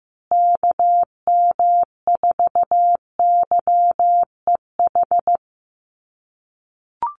Morse code, heard as a keyed tone, about 15 wpm, 700 Hz. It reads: KM4YEH
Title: USA Call Signs Repeated 2x 15wpm